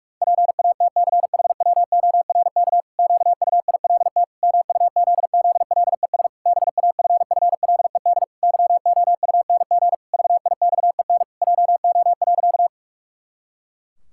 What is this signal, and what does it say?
Morse code, heard as a keyed tone, about 46 wpm, 700 Hz. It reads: PATCHWORK QUILT MUZZLES BAFFLED YOUNG VIXEN JO.
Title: Pangram Challenge v2 - 45wpm